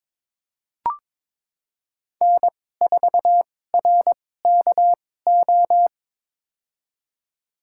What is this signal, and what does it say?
Morse code, heard as a keyed tone, about 22 wpm, spaced as for 16 wpm, 700 Hz. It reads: N4RKO
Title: USA Call Signs - 2x Character Spacing - 22wpm